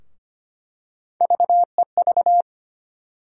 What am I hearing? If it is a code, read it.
Morse code, heard as a keyed tone, about 25 wpm, 700 Hz. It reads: VEV
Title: USA Call Sign Postfixes - 3 Letters 25wpm